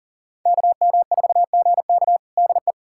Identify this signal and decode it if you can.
Morse code, heard as a keyed tone, about 40 wpm, 700 Hz. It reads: KM4GK DE
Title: USA Callsign DE Callsign 40wpm